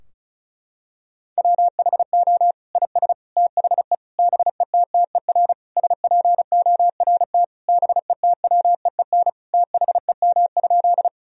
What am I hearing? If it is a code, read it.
Morse code, heard as a keyed tone, about 36 wpm, 700 Hz. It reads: WHO IS THE BETTER SPORT BETWEEN THEM?
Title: Sentences from Top 1000 Words - Review Part 4 - 35wpm